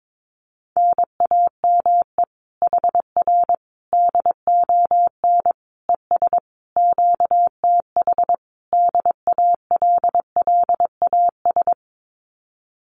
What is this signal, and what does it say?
Morse code, heard as a keyed tone, about 22 wpm, 700 Hz. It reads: NAME HR DON ES QTH DALLAS
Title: Common QSO Phrases Encore 22wpm